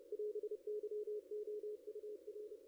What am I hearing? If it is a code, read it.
Morse code, heard as a keyed tone, about 31 wpm, 440 Hz. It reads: <AS>YOUR
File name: contests / ss-cw-k5tr-op-n5rz -2300